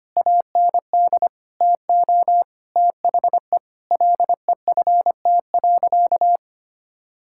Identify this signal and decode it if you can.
Morse code, heard as a keyed tone, about 25 wpm, 700 Hz. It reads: AND TO THE LEFT.
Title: Sentences from Top 1600 Words 25wpm